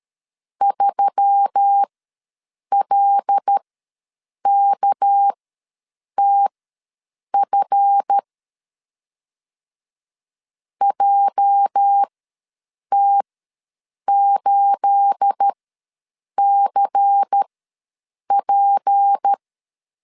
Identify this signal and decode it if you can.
Morse code, heard as a keyed tone, about 13 wpm, spaced as for 7 wpm, 800 Hz. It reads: 3LKTF JT8CP
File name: LEZIONI / 9 28 ws 12wpm 3354